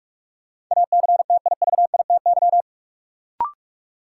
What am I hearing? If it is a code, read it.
Morse code, heard as a keyed tone, about 42 wpm, 700 Hz. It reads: ACTIVITY
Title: Sentences from Top 1000 Words - Review Part 4 - New Word Drill - 45wpm